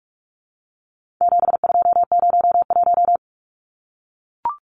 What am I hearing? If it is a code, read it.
Morse code, heard as a keyed tone, about 44 wpm, 700 Hz. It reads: 7201
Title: Numbers - 4-Digit 45wpm